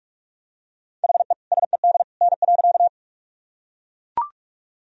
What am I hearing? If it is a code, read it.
Morse code, heard as a keyed tone, about 42 wpm, 700 Hz. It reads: HE SED N.
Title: Sentences from Top 100 Words and Abbreviations 45wpm